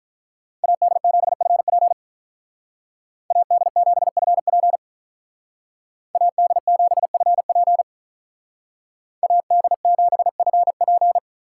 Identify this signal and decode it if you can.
Morse code, heard as a keyed tone, about 46 wpm, 700 Hz. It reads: AD7FP AD7FP AD7FP AD7FP